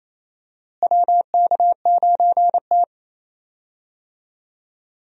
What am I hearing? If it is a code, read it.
Morse code, heard as a keyed tone, about 27 wpm, 700 Hz. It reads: WK9T
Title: USA Call Signs - 4 Characters - Repeated 2x - 28wpm